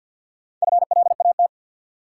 Morse code, heard as a keyed tone, about 46 wpm, 700 Hz. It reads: FLAT
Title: Sentences from Top 1000 Words - New Word Drill - 50wpm